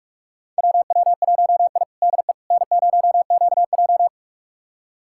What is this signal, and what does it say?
Morse code, heard as a keyed tone, about 43 wpm, 700 Hz. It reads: WW1I DE N0QJ
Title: USA Callsign DE Callsign - 4 Characters - 45wpm